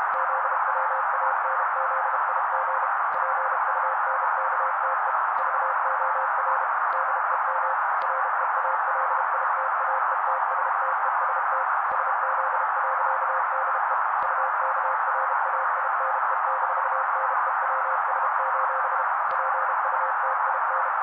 Morse code, heard as a keyed tone, about 32 wpm, 530 Hz. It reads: ZWANZIG PUNKTE VOR DEM REALITAETSTEST SZ.DE UKRAINE-NEWS: PUTIN
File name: daily / 31wpm noise